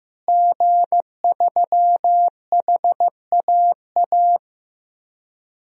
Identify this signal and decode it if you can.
Morse code, heard as a keyed tone, about 16 wpm, 700 Hz. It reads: G3HAA
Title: DXCC-100 Call Signs - 5 Characters - Repeated 3x - 15wpm